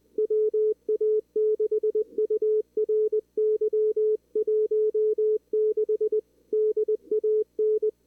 Morse code, heard as a keyed tone, about 18 wpm, 440 Hz. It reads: WA6URY 16 DAN
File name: CWSprint / Feb2022 / k5tr-feb-na-sprint-cw-k5ot -0000